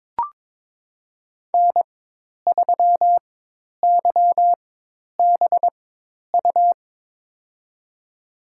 Morse code, heard as a keyed tone, about 22 wpm, spaced as for 10 wpm, 700 Hz. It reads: N3YBU
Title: USA Call Signs - 5 Characters - 4x Character Spacing - 22wpm